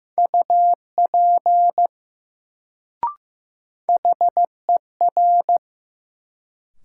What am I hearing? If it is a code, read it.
Morse code, heard as a keyed tone, about 15 wpm, 700 Hz. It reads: UP HER
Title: Sets of 2 Words - New Word Drill - 15wpm